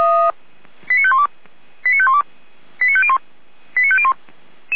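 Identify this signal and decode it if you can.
Five-tone selective call (ZVEI), heard as quick runs of five tones, tones of about 70 ms. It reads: 89512 89512 89501 89501
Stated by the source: ZVEI five-tone selective call received with an SDR.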